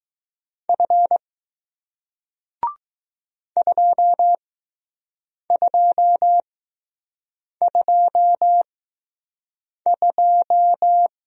Morse code, heard as a keyed tone, about 20 wpm, spaced as for 6 wpm, 700 Hz. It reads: F 2222